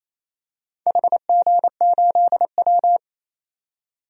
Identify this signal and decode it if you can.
Morse code, heard as a keyed tone, about 27 wpm, 690 Hz. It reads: HG8W
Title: DXCC-100 Call Signs - 4 Characters - 28wpm